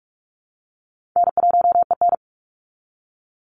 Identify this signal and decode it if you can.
Morse code, heard as a keyed tone, about 46 wpm, 700 Hz. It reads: N1EN
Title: Easy Call Signs - 45wpm